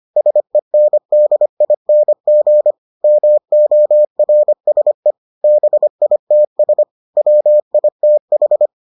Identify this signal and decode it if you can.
Morse code, heard as a keyed tone, about 25 wpm, 600 Hz. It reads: SENDING MORSE BITS WITH